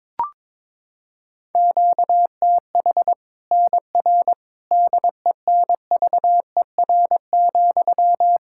Morse code, heard as a keyed tone, about 22 wpm, 700 Hz. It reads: QTH NR DENVER,
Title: K1USN SST QSO Practice 22wpm